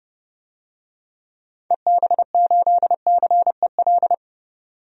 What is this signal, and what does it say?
Morse code, heard as a keyed tone, about 30 wpm, 700 Hz. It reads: EB8CEL
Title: DXCC-100 Call Signs - 6 Characters - 30wpm